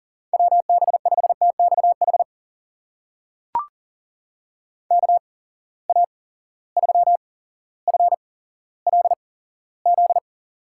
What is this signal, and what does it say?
Morse code, heard as a keyed tone, about 42 wpm, spaced as for 17 wpm, 700 Hz. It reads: WB5TXH KA3FLZ
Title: USA Call Signs - 6 Characters - 8x Character Spacing - 40wpm